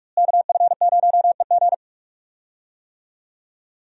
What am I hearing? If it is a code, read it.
Morse code, heard as a keyed tone, about 46 wpm, 700 Hz. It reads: KF0EG
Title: USA Call Signs - 5 Characters - Repeated 2x - 45wpm